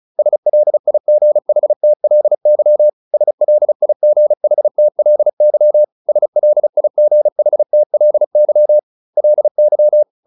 Morse code, heard as a keyed tone, about 34 wpm, 600 Hz. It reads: SLIGHTLY SLIGHTLY SLIGHTLY LY